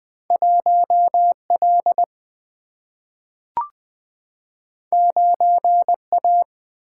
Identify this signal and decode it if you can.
Morse code, heard as a keyed tone, about 20 wpm, 700 Hz. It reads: 1 L 9 A
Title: Single-Letter Single-Number 20wpm